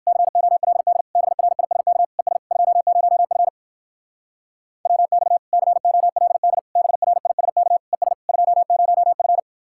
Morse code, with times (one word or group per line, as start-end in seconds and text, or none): 0.1-1.0 COLD
1.2-2.0 BRISK
2.2-2.4 ES
2.5-3.5 20F
4.9-5.4 WX
5.5-6.6 COLD
6.8-7.8 BRISK
7.9-8.1 ES
8.3-9.4 20F